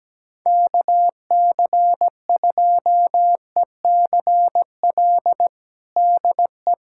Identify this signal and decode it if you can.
Morse code, heard as a keyed tone, about 17 wpm, 700 Hz. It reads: KC2ECL DE
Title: USA Callsign DE Callsign - 6 Characters - 17wpm